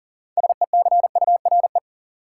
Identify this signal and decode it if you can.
Morse code, heard as a keyed tone, about 40 wpm, 700 Hz. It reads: SECURE
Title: Sentences from Top 1200 Words - New Word Drill Alt - 40wpm